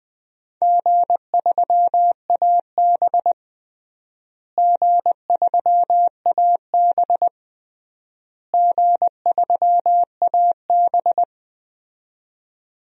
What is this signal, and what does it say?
Morse code, heard as a keyed tone, about 20 wpm, 700 Hz. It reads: G3AB G3AB G3AB
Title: DXCC-100 Call Signs - 4 Characters - Repeated 3x - 20wpm